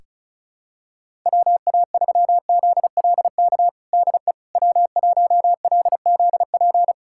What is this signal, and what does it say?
Morse code, heard as a keyed tone, about 36 wpm, 700 Hz. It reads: WA3ZLK DE W1LZP
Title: USA Callsign DE Callsign 35wpm